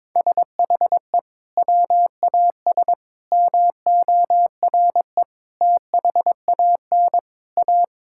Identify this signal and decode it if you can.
Morse code, heard as a keyed tone, about 22 wpm, 700 Hz. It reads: SHE WAS MORE THAN A
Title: Sentences from Top 1600 Words 22wpm